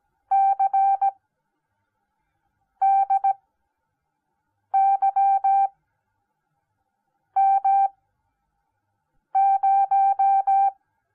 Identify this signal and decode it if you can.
Morse code, heard as a keyed tone, about 17 wpm, spaced as for 5 wpm, 780 Hz. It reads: CDYM0